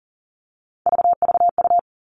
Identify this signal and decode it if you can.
Morse code, heard as a keyed tone, about 40 wpm, 700 Hz. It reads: VVU
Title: All Cut Numbers - 3-Digit 40wpm